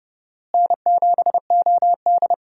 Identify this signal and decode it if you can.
Morse code, heard as a keyed tone, about 29 wpm, 700 Hz. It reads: N7OD
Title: USA Call Signs - 4 Characters - Repeated 3x - 30wpm